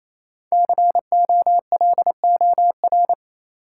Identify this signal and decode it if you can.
Morse code, heard as a keyed tone, about 28 wpm, 700 Hz. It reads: COLOR
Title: Sentences from Top 1000 Words - Review Part 2 - New Word Drill - 28wpm